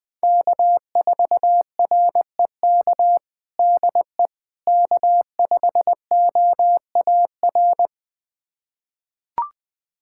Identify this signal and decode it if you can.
Morse code, heard as a keyed tone, about 20 wpm, 700 Hz. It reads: K4REK DE K5OAR
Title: USA Callsign DE Callsign - 5 Characters - 20wpm